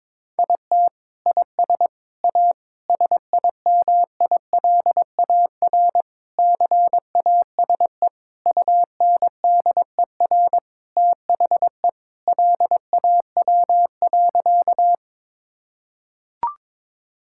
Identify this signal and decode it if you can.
Morse code, heard as a keyed tone, about 22 wpm, 700 Hz. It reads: IT IS A SIMILAR CASE UNDER THE LAW.
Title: Sentences from Top 1000 Words - Review Part 5 - 22wpm